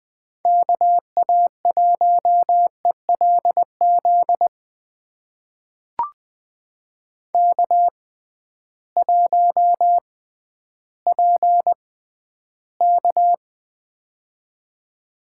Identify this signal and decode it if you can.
Morse code, heard as a keyed tone, about 20 wpm, spaced as for 12 wpm, 700 Hz. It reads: KA1ELZ K1PK
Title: USA Call Signs - 6x Character Spacing - 20wpm